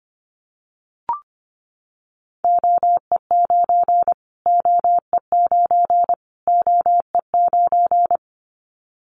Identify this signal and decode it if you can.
Morse code, heard as a keyed tone, about 25 wpm, 700 Hz. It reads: OE9 OE9 OE9
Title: Callsign Prefixes for Top 25 DXCC - Repeated 3x - 25wpm